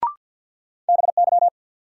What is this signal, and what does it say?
Morse code, heard as a keyed tone, about 48 wpm, 700 Hz. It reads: BY
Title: Sentences from Top 100 Words - New Word Drill - 50wpm